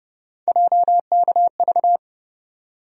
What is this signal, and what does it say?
Morse code, heard as a keyed tone, about 32 wpm, 700 Hz. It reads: JKV
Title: USA Call Sign Postfixes - 3 Letters 30wpm